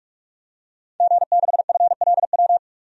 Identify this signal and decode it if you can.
Morse code, heard as a keyed tone, about 46 wpm, 700 Hz. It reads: G6FLW
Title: DXCC-100 Call Signs - 5 Characters - 45wpm